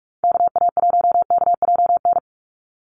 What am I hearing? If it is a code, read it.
Morse code, heard as a keyed tone, about 44 wpm, 700 Hz. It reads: KA1KJN
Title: USA Call Signs 45wpm